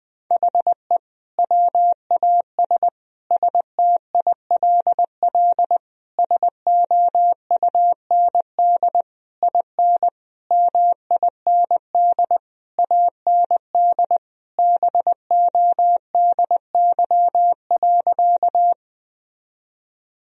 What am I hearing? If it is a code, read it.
Morse code, heard as a keyed tone, about 21 wpm, 700 Hz. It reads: HE WAS STILL SOUND IN MIND AND BODY.
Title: Sentences from Top 1000 Words - Review Part 2 - 20wpm